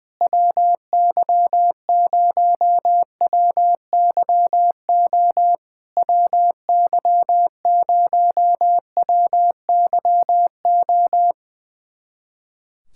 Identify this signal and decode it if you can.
Morse code, heard as a keyed tone, about 20 wpm, 700 Hz. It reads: WY0WYO WY0WYO
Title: SOTA QSO Practice 20wpm